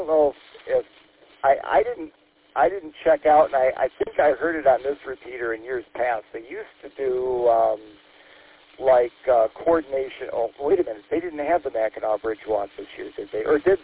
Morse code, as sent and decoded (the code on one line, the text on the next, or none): none